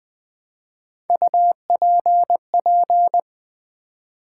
Morse code, heard as a keyed tone, about 20 wpm, 700 Hz. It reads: UPP
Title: Build A Word - 1 Letter At A Time - 3 to 5-Letter Words - 20wpm